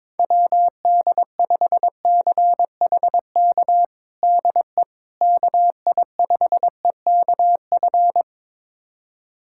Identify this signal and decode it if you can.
Morse code, heard as a keyed tone, about 22 wpm, 700 Hz. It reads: WD5CHK DE KI5EKF
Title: USA Callsign DE Callsign 22wpm